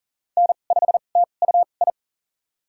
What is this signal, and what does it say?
Morse code, heard as a keyed tone, about 40 wpm, spaced as for 29 wpm, 700 Hz. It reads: N5TUI
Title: USA Call Signs - 2x Character Spacing - 40wpm